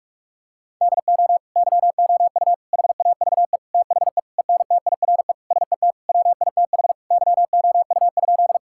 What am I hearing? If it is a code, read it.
Morse code, heard as a keyed tone, about 46 wpm, 700 Hz. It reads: DO YOU HAVE THE ENTIRE SET WITH YOU?
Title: Sentences from Top 1000 Words - Review Part 5 - 45wpm